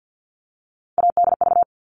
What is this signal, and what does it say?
Morse code, heard as a keyed tone, about 50 wpm, 700 Hz. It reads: ADV
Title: All Cut Numbers - 3-Digit 50wpm